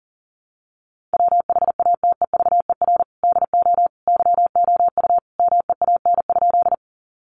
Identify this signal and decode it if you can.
Morse code, heard as a keyed tone, about 40 wpm, 700 Hz. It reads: WHATEVER DO YOU MEAN?
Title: Sentences from Top 1100 Words 40wpm